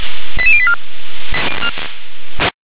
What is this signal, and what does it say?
Five-tone selective call (ZVEI), heard as quick runs of five tones, tones of about 70 ms. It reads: 88084
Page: /357/SDR/ZVEI/88740/2024.01.08/